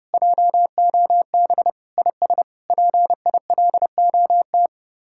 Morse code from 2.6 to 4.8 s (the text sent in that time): PILOT